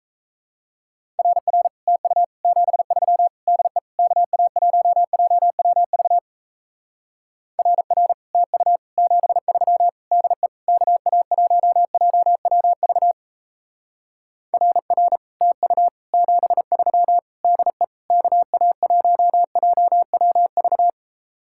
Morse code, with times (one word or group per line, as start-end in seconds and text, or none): 1.2-1.7 RR
1.9-2.2 TU
2.4-3.3 73
3.5-3.8 DE
4.0-6.2 KA1JWV
7.6-8.1 RR
8.3-8.8 TU
9.0-9.9 73
10.1-10.5 DE
10.7-13.1 KA1JWV
14.5-15.2 RR
15.4-15.9 TU
16.1-17.2 73
17.4-17.8 DE
18.1-20.9 KA1JWV